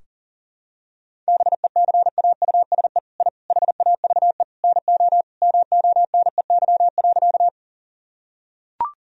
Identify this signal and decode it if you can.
Morse code, heard as a keyed tone, about 37 wpm, 700 Hz. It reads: BECAUSE I HAVE NO MONEY.
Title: Sentences from Top 1000 Words - Review Part 3 - 40wpm